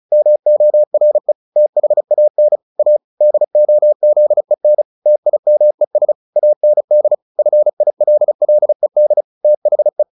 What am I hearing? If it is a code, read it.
Morse code, heard as a keyed tone, about 35 wpm, 600 Hz. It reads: MORE THAN A DOZEN TIMES AND FILLED THE
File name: the old man and the sea 2 35wpm